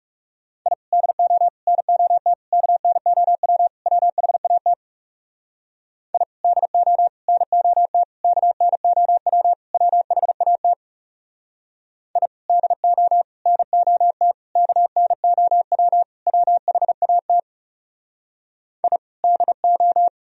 Morse code, heard as a keyed tone, about 37 wpm, 700 Hz. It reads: I DO NOT KNOW WHAT I DO NOT KNOW WHAT I DO NOT KNOW WHAT I DO
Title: Speed-Racing - Sets of 5 Words 30wpm